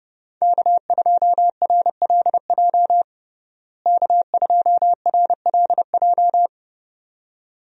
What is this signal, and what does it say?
Morse code, heard as a keyed tone, about 30 wpm, 700 Hz. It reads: K2RLJ K2RLJ
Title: USA Call Signs - 5 Characters - Repeated 2x - 30wpm